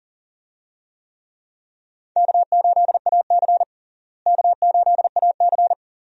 Morse code, heard as a keyed tone, about 40 wpm, 700 Hz. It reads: K8AC K8AC
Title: USA Call Signs - 4 Characters - Repeated 2x - 40wpm